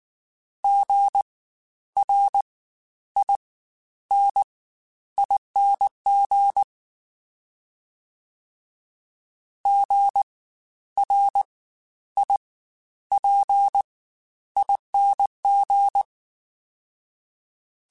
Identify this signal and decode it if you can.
Morse code, heard as a keyed tone, about 19 wpm, spaced as for 11 wpm, 800 Hz. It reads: GRINING GRIPING
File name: Spaced Alphas / Spaced Alphas mp3s / Gs sp @18